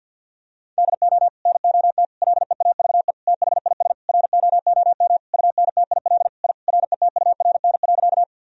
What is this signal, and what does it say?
Morse code, heard as a keyed tone, about 47 wpm, 700 Hz. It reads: DO NOT LEAVE THIS ROOM UNTIL I RETURN.
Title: Sentences from Top 1000 Words - Review Part 1 - 50wpm